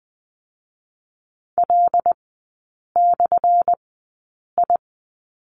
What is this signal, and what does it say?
Morse code, heard as a keyed tone, about 20 wpm, 700 Hz. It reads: L / I
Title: 218 Lesson 35e - Number 0 - Three Characters 20wpm